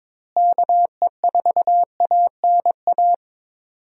Morse code, heard as a keyed tone, about 22 wpm, 700 Hz. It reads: KE4ANA
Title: USA Call Signs - 6 Characters - Repeated 3x - 22wpm